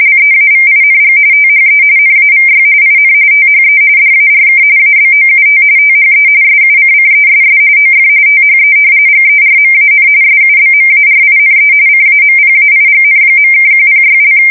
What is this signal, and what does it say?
RTTY, about 45.45 baud. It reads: LEAVING THE NAVY IN THE DARK ABOUT THE SUCCESS OF THE GERMAN SUBMARINE CAMPAIGN.(106)
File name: fsk170 / usnavy4